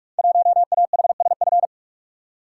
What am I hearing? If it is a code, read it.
Morse code, heard as a keyed tone, about 44 wpm, 700 Hz. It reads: 1AHSF
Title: Code Groups - 5 Letter:Numbers - 45wpm